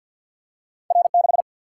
Morse code, heard as a keyed tone, about 50 wpm, 700 Hz. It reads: R 6
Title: Single-Letter Single-Number 50wpm